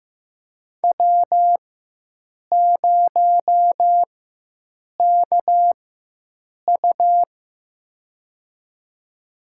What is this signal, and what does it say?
Morse code, heard as a keyed tone, about 15 wpm, spaced as for 7 wpm, 700 Hz. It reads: W0KU
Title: USA Call Signs - 4 Characters - 4x Character Spacing - 15wpm